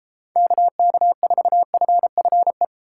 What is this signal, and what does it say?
Morse code, heard as a keyed tone, about 32 wpm, 700 Hz. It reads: KK4FFE